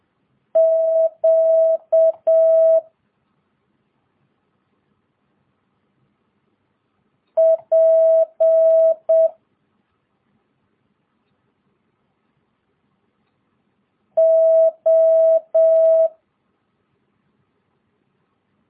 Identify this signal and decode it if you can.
Morse code, heard as a keyed tone, about 7 wpm, 650 Hz. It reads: Q P O
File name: jh8cbh / 2022cw